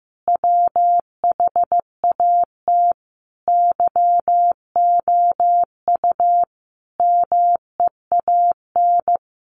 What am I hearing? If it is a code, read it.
Morse code, heard as a keyed tone, about 15 wpm, 700 Hz. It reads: WHAT YOU MEAN
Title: Sentences from Top 1000 Words 15wpm